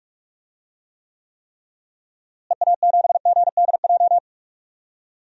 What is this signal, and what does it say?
Morse code, heard as a keyed tone, about 46 wpm, 700 Hz. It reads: EA7GDJ
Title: DXCC-100 Call Signs - 6 Characters - Repeated 3x - 45wpm